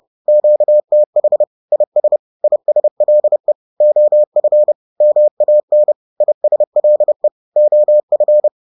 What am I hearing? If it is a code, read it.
Morse code, heard as a keyed tone, about 30 wpm, 600 Hz. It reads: QTH IS ISLE OF MAN ISLE OF